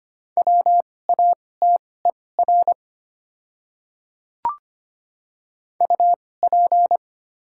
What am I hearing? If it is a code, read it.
Morse code, heard as a keyed tone, about 25 wpm, spaced as for 18 wpm, 700 Hz. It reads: WATER UP
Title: 2x Character Spacing - Top 100 Words - 25wpm